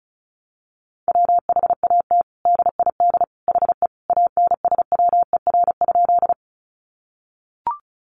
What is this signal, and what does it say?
Morse code, heard as a keyed tone, about 34 wpm, 700 Hz. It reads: WHAT DID HE ANSWER?
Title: Sentences from Top 300 Words 35wpm